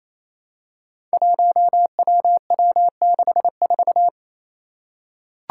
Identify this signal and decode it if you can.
Morse code, heard as a keyed tone, about 29 wpm, 700 Hz. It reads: 1WW64
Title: Five Challenging Letter-Number Combinations 28wpm